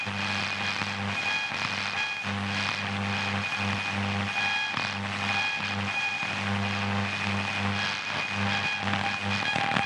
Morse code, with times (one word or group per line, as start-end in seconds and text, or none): none